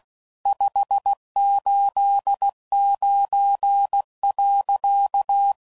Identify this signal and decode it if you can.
Morse code, heard as a keyed tone, about 16 wpm, 800 Hz. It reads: 589.
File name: CW Code Groups @15WPM / QSOs / WD0FBT Spaced